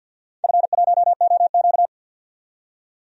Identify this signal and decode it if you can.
Morse code, heard as a keyed tone, about 46 wpm, 700 Hz. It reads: F1OQ
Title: DXCC-100 Call Signs - 4 Characters - 50wpm